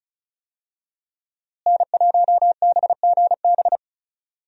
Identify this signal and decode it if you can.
Morse code, heard as a keyed tone, about 34 wpm, 700 Hz. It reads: N1BGB
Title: USA Call Signs - 5 Characters - Repeated 2x - 35wpm